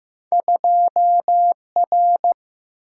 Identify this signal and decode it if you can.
Morse code, heard as a keyed tone, about 15 wpm, 700 Hz. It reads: 2R
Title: Single-Letter Single-Number 15wpm